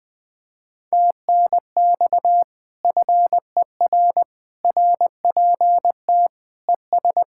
Morse code, heard as a keyed tone, about 20 wpm, 700 Hz. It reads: TNX FER RPT ES
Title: Speed-Racing Common QSO Phrases - 17wpm